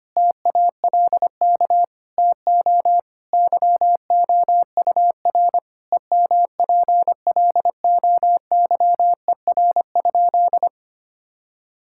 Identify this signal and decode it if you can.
Morse code, heard as a keyed tone, about 25 wpm, 700 Hz. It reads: TALK TO YOUR EMPLOYER?
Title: Sentences from Top 1600 Words 25wpm